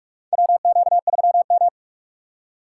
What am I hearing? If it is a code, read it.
Morse code, heard as a keyed tone, about 44 wpm, 700 Hz. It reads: WQ3M
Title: USA Call Signs - 4 Characters - 6x Character Spacing - 45wpm